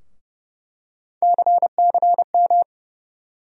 Morse code, heard as a keyed tone, about 30 wpm, 700 Hz. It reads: CCM
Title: USA Call Sign Postfixes - 3 Letters 30wpm